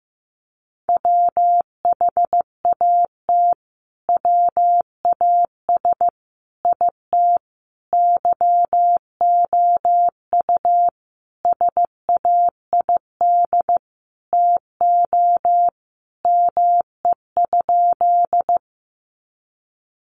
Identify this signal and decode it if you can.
Morse code, heard as a keyed tone, about 15 wpm, 700 Hz. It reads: WHAT WAS IT YOU SAID TO ME?
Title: Sentences from Top 200 Words 15wpm